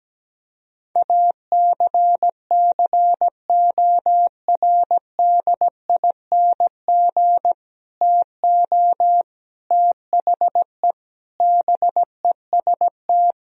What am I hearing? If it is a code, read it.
Morse code, heard as a keyed tone, about 17 wpm, 700 Hz. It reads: ACCORDING TO THE BEST
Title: Sentences from Top 1000 Words - Review Part 4 - 17wpm